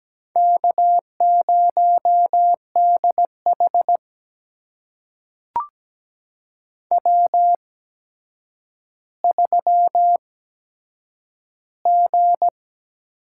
Morse code, heard as a keyed tone, about 17 wpm, spaced as for 8 wpm, 700 Hz. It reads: K0DH W3G
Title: USA Call Signs - 4 Characters - 8x Character Spacing - 17wpm